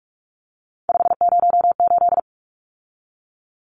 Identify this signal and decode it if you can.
Morse code, heard as a keyed tone, about 44 wpm, 700 Hz. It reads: 508
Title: Numbers - 3-Digit 45wpm